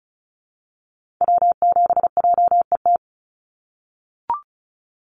Morse code, heard as a keyed tone, about 34 wpm, 700 Hz. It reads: W7JET
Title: NA SOTA Activator Callsigns 35wpm